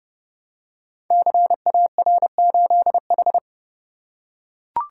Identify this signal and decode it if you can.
Morse code, heard as a keyed tone, about 30 wpm, 700 Hz. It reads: CAR8H
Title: Code Groups - 5 Letter:Numbers - 30wpm